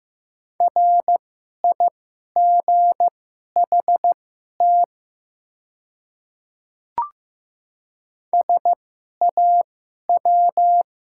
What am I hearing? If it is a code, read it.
Morse code, heard as a keyed tone, about 15 wpm, spaced as for 11 wpm, 700 Hz. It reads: RIGHT SAW
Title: 2x Character Spacing - Top 300 Words - 15wpm